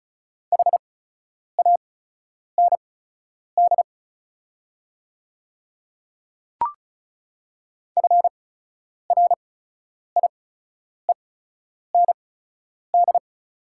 Morse code, heard as a keyed tone, about 36 wpm, spaced as for 10 wpm, 700 Hz. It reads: HAND FRIEND
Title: 8x Character Spacing - Top 500 Words - 35wpm